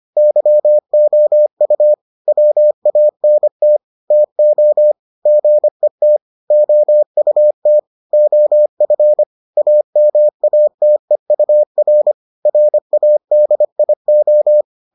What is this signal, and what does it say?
Morse code, heard as a keyed tone, about 25 wpm, 600 Hz. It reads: YOU WANT TO GET OUT OF AMATEUR RADIO